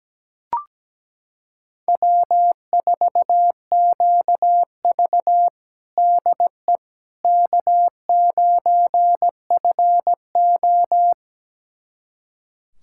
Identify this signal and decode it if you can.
Morse code, heard as a keyed tone, about 17 wpm, 700 Hz. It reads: W4QV DE K9FO
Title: USA Callsign DE Callsign - 4 Characters - 17wpm